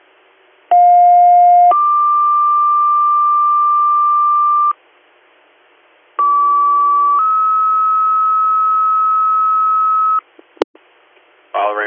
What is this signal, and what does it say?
Two-tone page, heard as a one-second tone, then a three-second tone, about 710 then 1180 Hz, about 1150 then 1290 Hz, with speech. It is heard from 0.7 to 10.2 s.